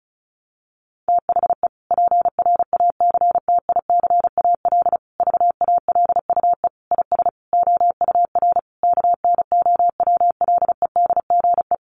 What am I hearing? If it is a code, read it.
Morse code, heard as a keyed tone, about 34 wpm, 700 Hz. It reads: THE PRACTICAL VALUE IS OUR KNOWLEDGE
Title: Sentences from Top 1400 Words 35wpm